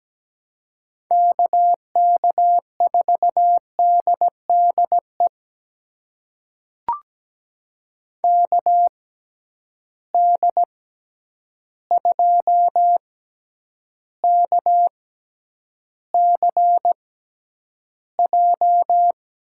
Morse code, heard as a keyed tone, about 17 wpm, spaced as for 9 wpm, 700 Hz. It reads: KK4DDE KD2KCJ
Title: USA Call Signs - 6 Characters - 6x Character Spacing - 17wpm